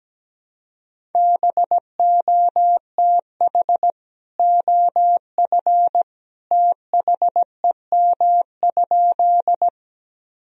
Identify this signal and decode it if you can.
Morse code, heard as a keyed tone, about 17 wpm, 700 Hz. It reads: BOTH OF THEM?
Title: Sentences from Top 1000 Words - Review Part 1 - 17wpm